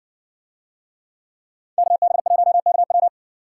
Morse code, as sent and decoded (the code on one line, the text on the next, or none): -.- -... .---- -.- .--
KB1KW